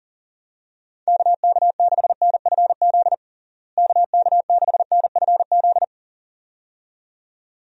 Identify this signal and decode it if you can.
Morse code, heard as a keyed tone, about 40 wpm, 700 Hz. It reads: KK6NFZ KK6NFZ
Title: USA Call Signs - 6 Characters - Repeated 2x - 40wpm